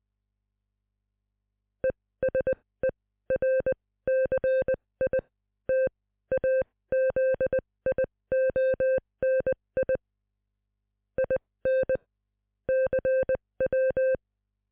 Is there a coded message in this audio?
Yes: Morse code from 1.8 to 14.2 s, heard as a keyed tone, about 20 wpm, spaced as for 15 wpm, 520 Hz. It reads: ESERCITAZIONI IN CW